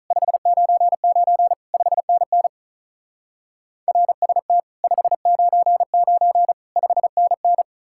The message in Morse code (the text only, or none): 599 5NN RST 599 5NN